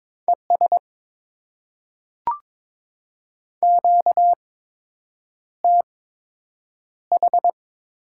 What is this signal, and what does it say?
Morse code, heard as a keyed tone, about 22 wpm, spaced as for 8 wpm, 700 Hz. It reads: ES QTH
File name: Top 20 QSO Elements - 8x Character Spacing - 22wpm